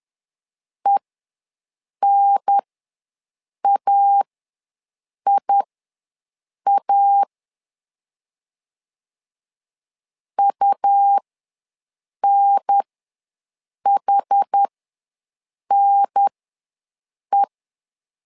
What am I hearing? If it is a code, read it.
Morse code, heard as a keyed tone, about 10 wpm, spaced as for 6 wpm, 800 Hz. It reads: ENAIA UNHNE
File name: LEZIONI / 2 UHDN / 2 UHDN ws 10wpm 2044